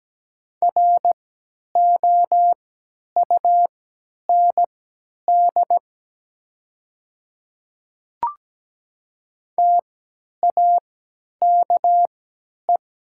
Morse code, heard as a keyed tone, about 17 wpm, spaced as for 10 wpm, 700 Hz. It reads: ROUND TAKE
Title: 3x Character Spacing - Top 200 Words - 17wpm